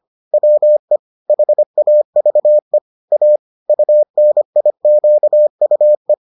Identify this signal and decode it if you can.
Morse code, heard as a keyed tone, about 25 wpm, 600 Hz. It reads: WE HAVE A UNIQUE